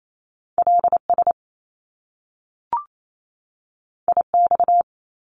Morse code, heard as a keyed tone, about 28 wpm, 700 Hz. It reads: LS IX
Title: Top Two Letter Combinations in Words 28wpm